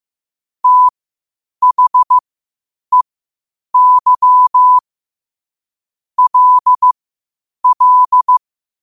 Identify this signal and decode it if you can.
Morse code, heard as a keyed tone, about 15 wpm, spaced as for 8 wpm, 1000 Hz. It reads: THEY LL
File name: qotdfiles / 8wpm